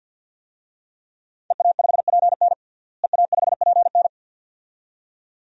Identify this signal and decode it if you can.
Morse code, heard as a keyed tone, about 49 wpm, 700 Hz. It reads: EA5PN EA5PN
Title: DXCC-100 Call Signs - 5 Characters - Repeated 2x - 50wpm